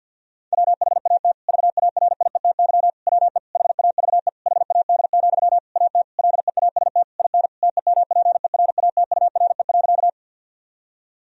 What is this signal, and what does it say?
Morse code, heard as a keyed tone, about 50 wpm, 700 Hz. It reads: WHAT VARIETY WE HAVE HAD, AT LEAST IN TEMPERATURE.
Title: Sentences from Top 1000 Words - Review Part 4 - 50wpm